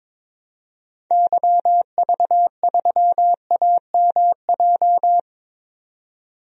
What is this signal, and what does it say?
Morse code, heard as a keyed tone, about 22 wpm, 700 Hz. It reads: YV3AMJ
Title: DXCC-100 Call Signs - 6 Characters - 22wpm